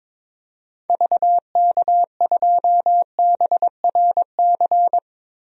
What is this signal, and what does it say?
Morse code, heard as a keyed tone, about 22 wpm, 700 Hz. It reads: VK2BRC
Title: DXCC-100 Call Signs - 6 Characters - Repeated 3x - 22wpm